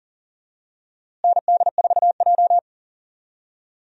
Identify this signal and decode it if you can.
Morse code, heard as a keyed tone, about 40 wpm, 700 Hz. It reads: ND4J
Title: USA Call Signs - 4 Characters - Repeated 3x - 40wpm